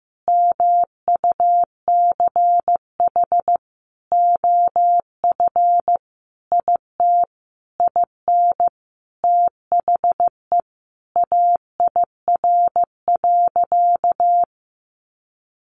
Morse code, heard as a keyed tone, about 15 wpm, 700 Hz. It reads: MUCH OF IT IN THE AIR.
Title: Sentences from Top 900 Words 15wpm